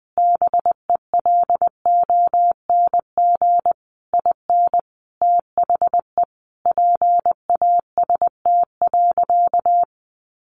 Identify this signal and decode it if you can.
Morse code, heard as a keyed tone, about 20 wpm, 700 Hz. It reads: BELONG IN THE PAST.
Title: A Sound Of Thunder 20wpm